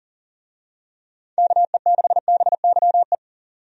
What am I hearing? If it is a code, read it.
Morse code, heard as a keyed tone, about 40 wpm, 700 Hz. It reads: KE6BYE
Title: USA Call Signs - 6 Characters - 40wpm